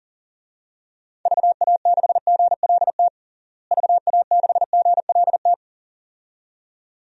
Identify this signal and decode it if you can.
Morse code, heard as a keyed tone, about 37 wpm, 700 Hz. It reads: VA6GLT VA6GLT
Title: DXCC-100 Call Signs - 6 Characters - Repeated 2x - 40wpm